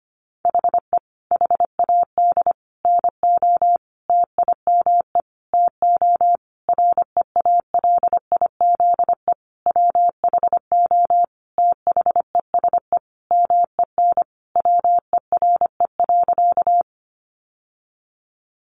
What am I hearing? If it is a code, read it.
Morse code, heard as a keyed tone, about 25 wpm, 700 Hz. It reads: HE HAD NO TIME TO REALIZE WHO THESE MEN WERE.
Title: Sentences from Top 1000 Words 25wpm